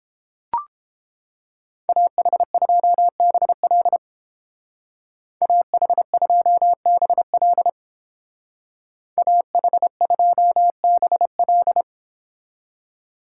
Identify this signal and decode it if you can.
Morse code, heard as a keyed tone, about 29 wpm, 700 Hz. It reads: AH2BL AH2BL AH2BL